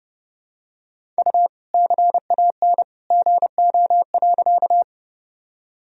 Morse code, heard as a keyed tone, about 31 wpm, 700 Hz. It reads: U CAN GO.
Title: Sentences from Top 100 Words and Abbreviations 30wpm